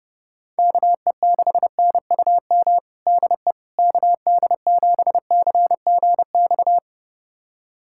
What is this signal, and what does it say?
Morse code, heard as a keyed tone, about 30 wpm, 700 Hz. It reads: KE6NUM DE KD7CGX
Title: USA Callsign DE Callsign - 6 Characters - 30wpm